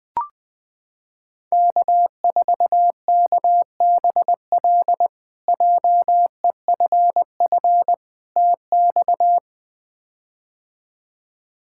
Morse code, heard as a keyed tone, about 20 wpm, 700 Hz. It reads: K4KBL JEFF TX
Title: K1USN SST QSO Practice - 20wpm